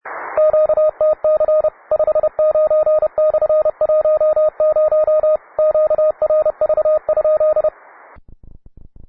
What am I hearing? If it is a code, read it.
Morse code, heard as a keyed tone, about 31 wpm, 620 Hz, with radio noise. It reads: QTC 59/10 QRV?